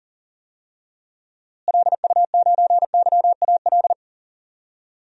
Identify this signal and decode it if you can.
Morse code, heard as a keyed tone, about 37 wpm, 700 Hz. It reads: LU9YAL
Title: DXCC-100 Call Signs - 6 Characters - Repeated 3x - 40wpm